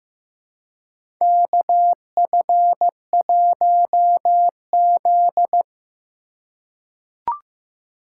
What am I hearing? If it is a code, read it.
Morse code, heard as a keyed tone, about 15 wpm, 700 Hz. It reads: KF1Z
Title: USA Call Signs - 4 Characters - 4x Character Spacing - 15wpm